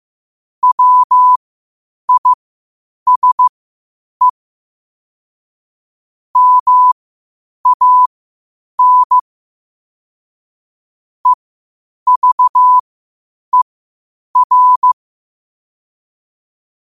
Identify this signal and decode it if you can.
Morse code, heard as a keyed tone, about 15 wpm, spaced as for 8 wpm, 1000 Hz. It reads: WISE MAN EVER
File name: qotdfiles / 8wpm